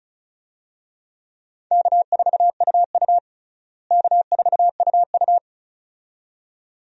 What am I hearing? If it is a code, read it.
Morse code, heard as a keyed tone, about 36 wpm, 700 Hz. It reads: K4UU K4UU
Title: USA Call Signs - 4 Characters - Repeated 2x - 35wpm